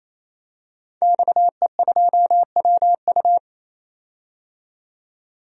Morse code, heard as a keyed tone, about 27 wpm, 700 Hz. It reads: XE2WU
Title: DXCC-100 Call Signs - 5 Characters - 28wpm